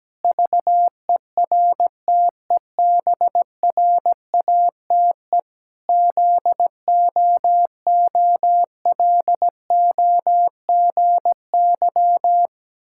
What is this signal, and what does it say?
Morse code, heard as a keyed tone, about 17 wpm, 700 Hz. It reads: VERTEBRATE ZOOLOGY
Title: Pangram Challenge v2 - 17wpm